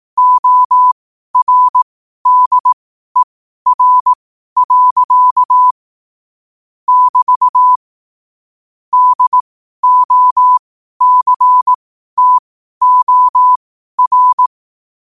Morse code, heard as a keyed tone, about 18 wpm, spaced as for 12 wpm, 1000 Hz. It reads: ORDER. <BT> DOCTOR